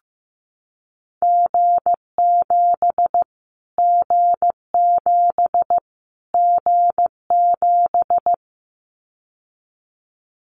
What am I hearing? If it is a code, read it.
Morse code, heard as a keyed tone, about 15 wpm, 700 Hz. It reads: G7 G7 G7
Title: Callsign Prefixes for Top 25 DXCC - Repeated 3x - 15wpm